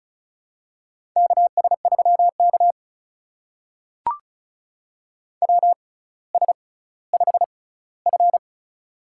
Morse code, heard as a keyed tone, about 36 wpm, spaced as for 18 wpm, 700 Hz. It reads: KS3K WS5F
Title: USA Call Signs - 4 Characters - 6x Character Spacing - 35wpm